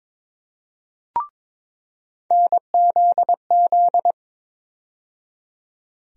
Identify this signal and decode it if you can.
Morse code, heard as a keyed tone, about 22 wpm, 700 Hz. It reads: NZZ
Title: USA Call Sign Postfixes - 3 Letters 22wpm